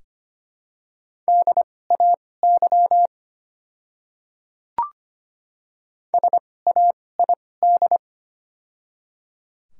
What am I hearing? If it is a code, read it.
Morse code, heard as a keyed tone, about 25 wpm, spaced as for 18 wpm, 700 Hz. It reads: DAY SAID
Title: 2x Character Spacing - Top 100 Words - 25wpm